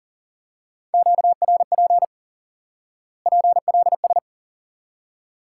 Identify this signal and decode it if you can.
Morse code, heard as a keyed tone, about 40 wpm, 700 Hz. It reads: QRP PLS
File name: course / instant-qso-element-course-3 / IQE Course 3 - Lesson 89 - YAGI - Two Elements - 40wpm